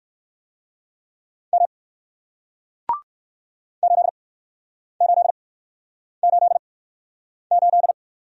Morse code, heard as a keyed tone, about 55 wpm, spaced as for 9 wpm, 700 Hz. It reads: M 8888
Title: Speed-Racing Single Letter-Number 45wpm